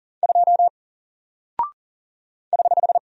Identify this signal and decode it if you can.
Morse code, heard as a keyed tone, about 40 wpm, 700 Hz. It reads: 2 <HH>
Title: Single Character (Letter:Number:Punctuation:Prosign) Alt 40wpm